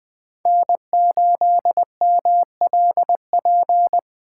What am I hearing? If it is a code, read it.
Morse code, heard as a keyed tone, about 20 wpm, 700 Hz. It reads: N8MLP
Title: USA Call Signs - 5 Characters - 4x Character Spacing - 20wpm